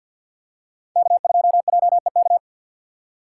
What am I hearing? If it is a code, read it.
Morse code, heard as a keyed tone, about 48 wpm, 700 Hz. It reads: K2JEK
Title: USA Call Signs - 5 Characters - 3x Character Spacing - 50wpm